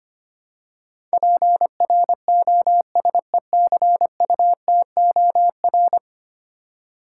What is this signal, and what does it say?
Morse code, heard as a keyed tone, about 25 wpm, 700 Hz. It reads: PROSECUTOR
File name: 10 Letter Words - First 3 Letters Given - 25wpm